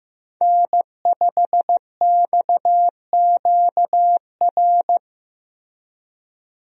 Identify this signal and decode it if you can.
Morse code, heard as a keyed tone, about 15 wpm, 700 Hz. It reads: N5XQR
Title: USA Call Signs - 5 Characters - Repeated 2x - 15wpm